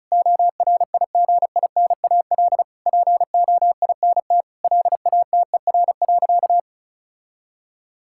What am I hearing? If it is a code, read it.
Morse code, heard as a keyed tone, about 36 wpm, 690 Hz. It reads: ORIGINAL POINT LATER.
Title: Sentences from Top 1000 Words - Review Part 6 - 35wpm